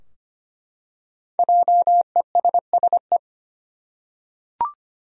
Morse code, heard as a keyed tone, about 25 wpm, 700 Hz. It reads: JESSE
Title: USA Common Names - Max 5 Letters - 25wpm